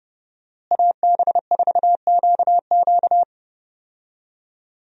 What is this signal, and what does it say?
Morse code, heard as a keyed tone, about 32 wpm, 700 Hz. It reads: AB4QQ
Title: USA Call Signs - 5 Characters - Repeated 2x - 30wpm